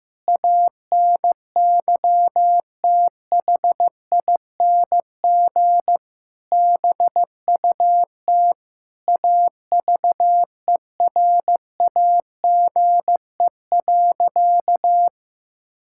Morse code, heard as a keyed tone, about 15 wpm, 700 Hz. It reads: ANYTHING BUT AVERAGE.
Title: Sentences from Top 1000 Words - Review Part 6 - 15wpm